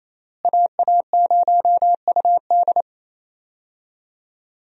Morse code, heard as a keyed tone, about 27 wpm, 700 Hz. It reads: AA0UD
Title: USA Call Signs - 5 Characters - Repeated 3x - 28wpm